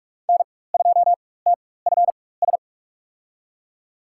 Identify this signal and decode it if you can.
Morse code, heard as a keyed tone, about 46 wpm, spaced as for 21 wpm, 700 Hz. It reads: N2TFS
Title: USA Call Signs - 5 Characters - 4x Character Spacing - 45wpm